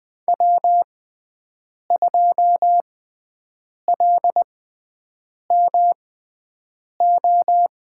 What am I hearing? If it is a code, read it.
Morse code, heard as a keyed tone, about 20 wpm, spaced as for 7 wpm, 700 Hz. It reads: W2LMO
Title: USA Call Signs - 5 Characters - 6x Character Spacing - 20wpm